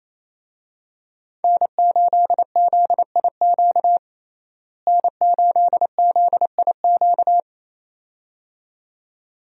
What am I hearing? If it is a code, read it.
Morse code, heard as a keyed tone, about 28 wpm, 700 Hz. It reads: N8ZIQ N8ZIQ
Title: USA Call Signs - 5 Characters - Repeated 2x - 28wpm